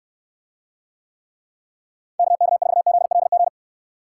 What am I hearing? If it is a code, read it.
Morse code, heard as a keyed tone, about 66 wpm, 700 Hz. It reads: KK4ZLG